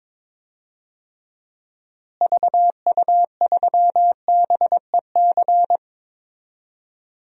Morse code, heard as a keyed tone, about 22 wpm, 700 Hz. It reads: VU3BEC
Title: DXCC-100 Call Signs - 6 Characters - Repeated 2x - 22wpm